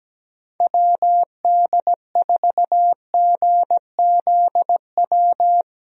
Morse code, heard as a keyed tone, about 17 wpm, 700 Hz. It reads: WD4GZW
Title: USA Call Signs - 6 Characters - Repeated 3x - 17wpm